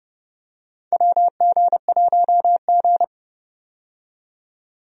Morse code, heard as a keyed tone, about 30 wpm, 700 Hz. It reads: WG1G
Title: USA Call Signs - 4 Characters - 30wpm